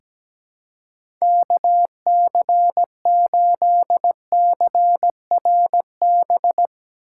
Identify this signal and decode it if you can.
Morse code, heard as a keyed tone, about 17 wpm, 700 Hz. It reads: KC8CRB
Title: USA Call Signs - 6 Characters - 4x Character Spacing - 17wpm